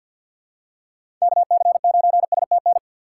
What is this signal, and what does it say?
Morse code, heard as a keyed tone, about 46 wpm, 700 Hz. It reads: KC9STN
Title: USA Call Signs - 3x Character Spacing - 50wpm